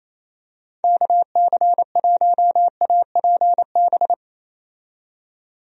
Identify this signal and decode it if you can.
Morse code, heard as a keyed tone, about 28 wpm, 700 Hz. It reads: KC1APB
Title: USA Call Signs - 6 Characters - 28wpm